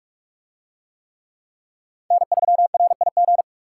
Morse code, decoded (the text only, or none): N3RIG